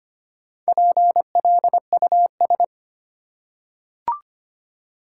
Morse code, heard as a keyed tone, about 25 wpm, 700 Hz. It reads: PLUS
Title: Four Letter Words - 25wpm